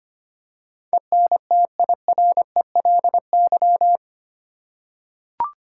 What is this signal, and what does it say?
Morse code, heard as a keyed tone, about 25 wpm, 700 Hz. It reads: ENTIRELY
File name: 8 Letter Words - First 3 Letters Given - 25wpm